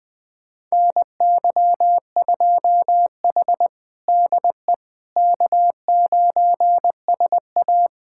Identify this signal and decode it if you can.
Morse code, heard as a keyed tone, about 20 wpm, 700 Hz. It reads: NY2H DE K9SA
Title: USA Callsign DE Callsign - 4 Characters - 20wpm